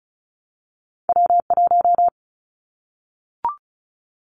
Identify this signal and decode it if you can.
Morse code, heard as a keyed tone, about 34 wpm, 700 Hz. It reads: W1
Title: Callsign Prefixes for Top 25 DXCC - Repeated 3x - 35wpm